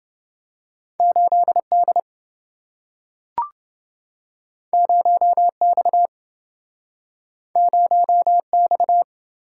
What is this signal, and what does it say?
Morse code, heard as a keyed tone, about 27 wpm, 700 Hz. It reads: 8D 0X 0X